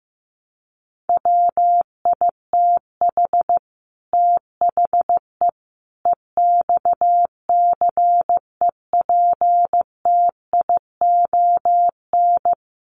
Morse code, heard as a keyed tone, about 15 wpm, 700 Hz. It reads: WITH THE EXCEPTION
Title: Sets of 4 Words 15wpm